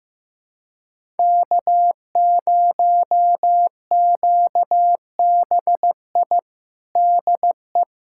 Morse code, heard as a keyed tone, about 16 wpm, 700 Hz. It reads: K0QBI DE
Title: USA Callsign DE Callsign - 5 Characters - 15wpm